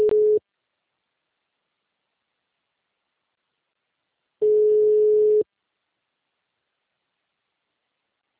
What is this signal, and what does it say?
Ringback tone (Europe, 425 Hz).